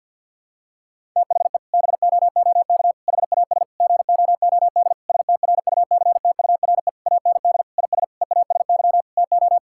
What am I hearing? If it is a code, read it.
Morse code, heard as a keyed tone, about 50 wpm, 700 Hz. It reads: THE BOOK HAS GOOD STRUCTURE AND IS EASY TO
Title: Sentences from Top 1000 Words - Review Part 6 - 50wpm